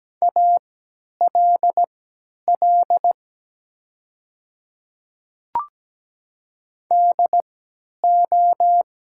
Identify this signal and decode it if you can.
Morse code, heard as a keyed tone, about 17 wpm, spaced as for 10 wpm, 700 Hz. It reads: ALL DO
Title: 3x Character Spacing - Top 100 Words - 17wpm